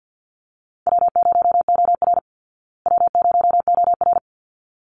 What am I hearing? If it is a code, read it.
Morse code, heard as a keyed tone, about 48 wpm, 700 Hz. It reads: W0OR W0OR
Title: Call Signs Repeated 3x - 50wpm